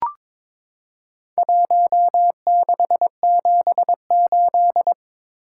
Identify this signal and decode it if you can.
Morse code, heard as a keyed tone, about 22 wpm, 700 Hz. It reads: 1678
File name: practice / 22wpm / Four Letter-Number Combinations - 22wpm